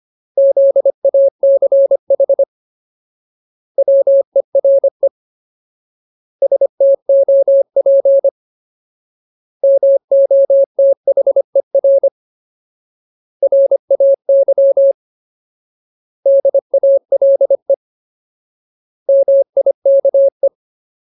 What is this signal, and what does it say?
Morse code, heard as a keyed tone, about 25 wpm, 550 Hz. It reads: ZACH WERE STOP MOTHER RAY DALE MIKE
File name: Ultimate Word List / 25WPM Ultimate